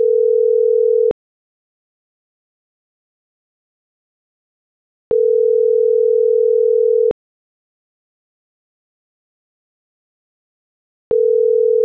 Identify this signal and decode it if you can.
Ringback tone (North America), 440 + 480 Hz.